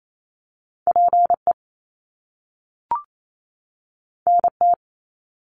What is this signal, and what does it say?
Morse code, heard as a keyed tone, about 28 wpm, 700 Hz. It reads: PE NT
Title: Canadian Provinces and Territories 28wpm